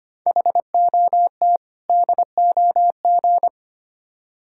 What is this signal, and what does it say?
Morse code, heard as a keyed tone, about 30 wpm, 700 Hz. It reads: HOT DOG
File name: Easy Words - Speed-Racing with Hints - 25wpm